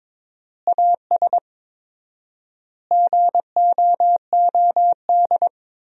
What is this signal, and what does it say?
Morse code, heard as a keyed tone, about 22 wpm, 700 Hz. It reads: AS GOOD
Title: 4x Word-Spacing - Sentences from Top 200 Words - 22wpm